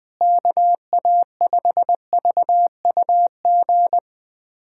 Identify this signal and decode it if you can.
Morse code, heard as a keyed tone, about 20 wpm, 700 Hz. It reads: KA5VUG
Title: USA Callsign DE Callsign - 6 Characters - 20wpm